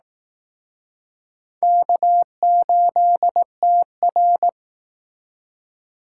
Morse code, heard as a keyed tone, about 19 wpm, 700 Hz. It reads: K8TR